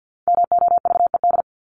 Morse code, heard as a keyed tone, about 50 wpm, 700 Hz. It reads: MOVED
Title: ED Suffix - Top 100 Words with Base Words 50wpm